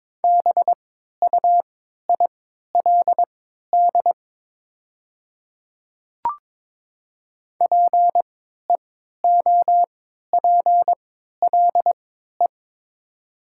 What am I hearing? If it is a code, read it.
Morse code, heard as a keyed tone, about 22 wpm, spaced as for 12 wpm, 700 Hz. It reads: BUILD PEOPLE
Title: 3x Character Spacing - Top 300 Words - 22wpm